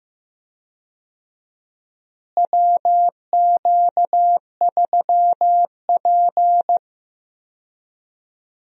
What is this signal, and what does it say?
Morse code, heard as a keyed tone, about 15 wpm, 700 Hz. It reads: WQ3P
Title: USA Call Signs - 4 Characters - Repeated 3x - 15wpm